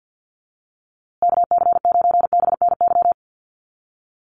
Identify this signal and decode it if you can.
Morse code, heard as a keyed tone, about 50 wpm, 700 Hz. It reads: KC9BNY
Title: USA Call Signs Repeated 2x 50wpm